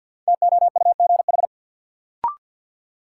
Morse code, heard as a keyed tone, about 51 wpm, 700 Hz. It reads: TOUGH
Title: Sentences from Top 1200 Words - New Word Drill Alt - 50wpm